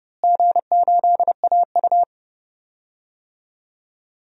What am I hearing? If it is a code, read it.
Morse code, heard as a keyed tone, about 30 wpm, 700 Hz. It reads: G8AU
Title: DXCC-100 Call Signs - 4 Characters - Repeated 2x - 30wpm